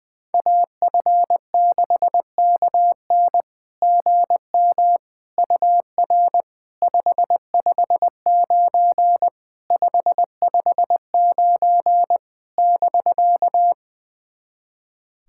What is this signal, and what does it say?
Morse code, heard as a keyed tone, about 20 wpm, 690 Hz. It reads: AF6KN GM UR 559 559 <BK>
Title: SOTA QSO Practice 20wpm